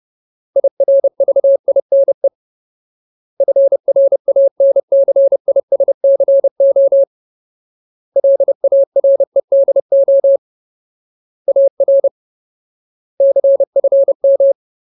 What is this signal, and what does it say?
Morse code, heard as a keyed tone, about 30 wpm, 550 Hz. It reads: IRVINE FRANCISCO LAREDO AR CFM
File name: Ultimate Word List / 30WPM Ultimate